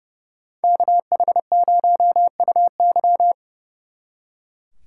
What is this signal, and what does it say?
Morse code, heard as a keyed tone, about 36 wpm, 700 Hz. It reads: KH0UY
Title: Speed Racing - USA 2x2 Call Signs 30wpm